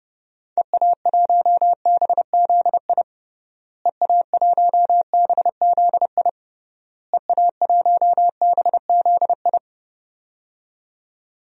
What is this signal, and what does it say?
Morse code, heard as a keyed tone, about 30 wpm, 700 Hz. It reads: EA1BZI EA1BZI EA1BZI
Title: DXCC-100 Call Signs - 6 Characters - Repeated 3x - 30wpm